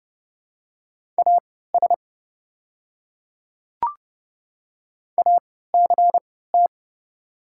Morse code, heard as a keyed tone, about 31 wpm, spaced as for 17 wpm, 700 Hz. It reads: AS ACT
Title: 3x Character Spacing - Top 500 Words - 30wpm